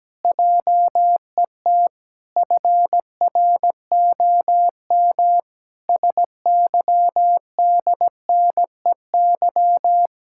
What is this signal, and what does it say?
Morse code, heard as a keyed tone, about 17 wpm, 700 Hz. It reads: JET FROM SYDNEY
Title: Pangram Challenge v2 - 17wpm